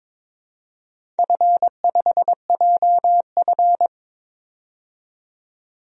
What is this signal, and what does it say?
Morse code, heard as a keyed tone, about 22 wpm, 700 Hz. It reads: F5JF
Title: DXCC-100 Call Signs - 4 Characters - 22wpm